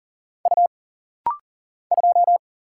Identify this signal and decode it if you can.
Morse code, heard as a keyed tone, about 40 wpm, 700 Hz. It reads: U 2
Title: Single Letter-Number Alt - Rapid-Fire 40wpm